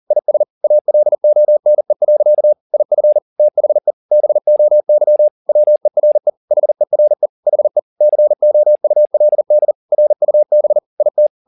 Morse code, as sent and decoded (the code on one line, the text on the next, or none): .. ... / .- .-.. --- -. .-.-.- / .. ..-. / - .... / -... --- -.-- / .-- . .-. / .... .-. / .... / -.-. --- ..- .-.. -.. / .-. ..- -... / .. -
IS ALONE. IF THE BOY WERE HERE HE COULD RUB IT